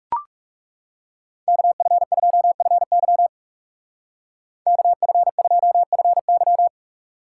Morse code, heard as a keyed tone, about 37 wpm, 700 Hz. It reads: KF2FY KF2FY
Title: Speed Racing - USA 2x2 Call Signs 30wpm